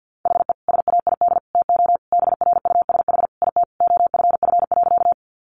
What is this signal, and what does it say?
Morse code, heard as a keyed tone, about 50 wpm, 700 Hz. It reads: HE SAID TO BRUSH IT OFF.
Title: Sentences from Top 1100 Words 50wpm